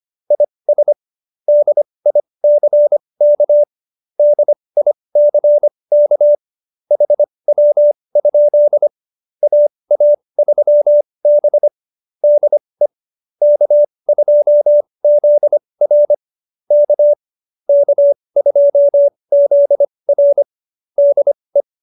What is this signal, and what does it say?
Morse code, heard as a keyed tone, about 25 wpm, spaced as for 20 wpm, 590 Hz. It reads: IS DICK DICK HW? AA3B DE K2ZR K K2ZR DE
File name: QSO302 20